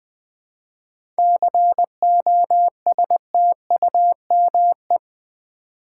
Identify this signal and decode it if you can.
Morse code, heard as a keyed tone, about 20 wpm, 700 Hz. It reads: COSTUME
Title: 7 Letter Words - First 3 Letters Given - 20wpm